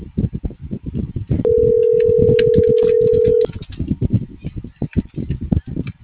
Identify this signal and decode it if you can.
Ringback tone (North America), 440 + 480 Hz.